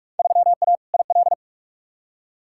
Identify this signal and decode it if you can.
Morse code, heard as a keyed tone, about 44 wpm, 700 Hz. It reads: 3A IL
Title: Field Day Exchanges - 45wpm